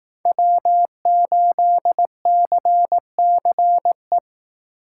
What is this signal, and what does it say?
Morse code, heard as a keyed tone, about 18 wpm, 700 Hz. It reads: W8CCE
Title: Speed Racing - USA 1x3 Call Signs 15wpm